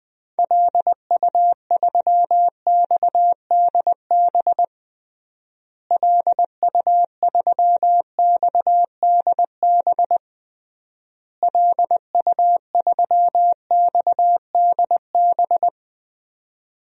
Morse code, heard as a keyed tone, about 20 wpm, 700 Hz. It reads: LU3XDB LU3XDB LU3XDB
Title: DXCC-100 Call Signs - 6 Characters - Repeated 3x - 20wpm